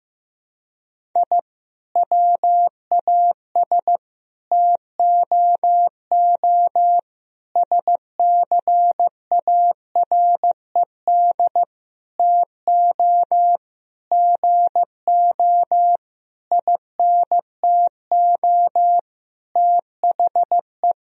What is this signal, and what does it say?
Morse code, heard as a keyed tone, about 15 wpm, 700 Hz. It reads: I WAS TOO SCARED TO GO INTO THE
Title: Sentences from Top 1000 Words - Review Part 4 - 15wpm